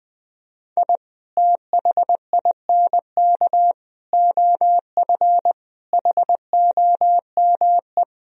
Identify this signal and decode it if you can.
Morse code, heard as a keyed tone, about 19 wpm, 700 Hz. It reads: I THINK OF HOME
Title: Sentences from Top 1000 Words - Review Part 3 - 20wpm